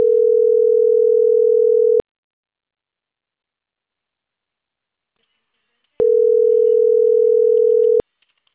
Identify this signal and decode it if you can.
Ringback tone (North America), 440 + 480 Hz.